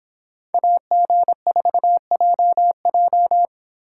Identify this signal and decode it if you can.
Morse code, heard as a keyed tone, about 26 wpm, 700 Hz. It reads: AG4JJ